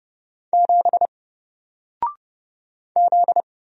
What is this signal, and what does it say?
Morse code, heard as a keyed tone, about 30 wpm, 700 Hz. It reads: 7 Z
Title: Single Letter-Number 30wpm